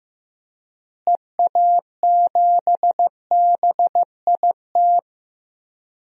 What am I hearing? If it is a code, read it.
Morse code, heard as a keyed tone, about 15 wpm, 700 Hz. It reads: EA7BIT
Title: DXCC-100 Call Signs - 6 Characters - Repeated 2x - 15wpm